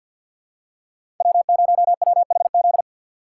Morse code, heard as a keyed tone, about 50 wpm, 700 Hz. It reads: W0WHZ
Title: USA Call Signs - 5 Characters - 3x Character Spacing - 50wpm